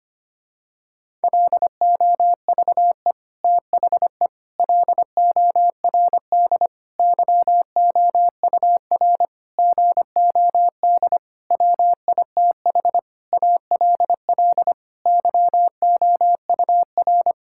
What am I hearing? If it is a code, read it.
Morse code, heard as a keyed tone, about 25 wpm, 700 Hz. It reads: LOVE THE LORD YOUR GOD WITH ALL YOUR
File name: practice / 25wpm / Sentences with 9-Word Phrases - 25wpm